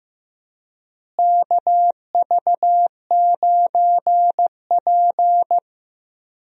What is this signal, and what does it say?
Morse code, heard as a keyed tone, about 15 wpm, 700 Hz. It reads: KV9P
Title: USA Call Signs - 4 Characters - Repeated 3x - 15wpm